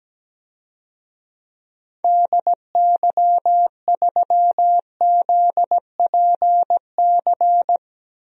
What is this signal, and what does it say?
Morse code, heard as a keyed tone, about 17 wpm, 700 Hz. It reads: DY3ZPC
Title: DXCC-100 Call Signs - 6 Characters - Repeated 3x - 17wpm